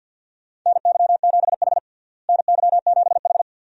Morse code, heard as a keyed tone, about 53 wpm, 700 Hz. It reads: NY7H NY7H
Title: USA Call Signs - 4 Characters - Repeated 2x - 50wpm